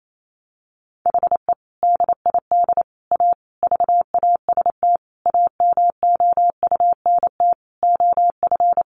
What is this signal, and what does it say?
Morse code, heard as a keyed tone, about 28 wpm, 700 Hz. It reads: HE DID A VAST AMOUNT OF
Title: Sentences from Top 1300 Words 28wpm